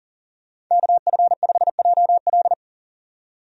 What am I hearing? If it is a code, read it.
Morse code, heard as a keyed tone, about 40 wpm, 700 Hz. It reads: KF5JL
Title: USA Call Signs - 5 Characters - 40wpm